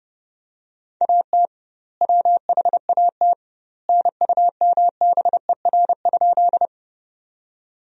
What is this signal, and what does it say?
Morse code, heard as a keyed tone, about 30 wpm, 700 Hz. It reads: AT WHAT NUMBER?
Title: 2x Word-Spacing - Sentences from Top 100 Words - 30wpm